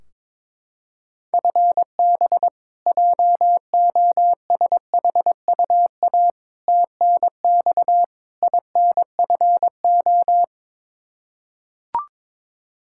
Morse code, heard as a keyed tone, about 22 wpm, 700 Hz. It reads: FB JOSHUA TNX INFO
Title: K1USN SST QSO Practice 22wpm